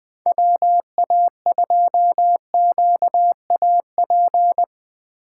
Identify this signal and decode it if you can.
Morse code, heard as a keyed tone, about 20 wpm, 700 Hz. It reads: WA2QAP
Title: USA Call Signs - 6 Characters - 2x Character Spacing - 20wpm